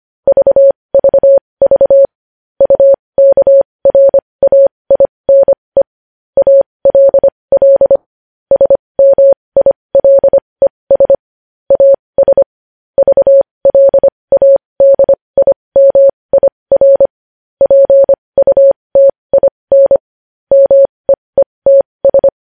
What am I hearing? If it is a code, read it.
Morse code, heard as a keyed tone, about 25 wpm, spaced as for 20 wpm, 550 Hz. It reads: VVV UKRAINE ALL SMILES AS VLADIMIR PUTIN MEETS
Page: news headlines in morse code at 20 wpm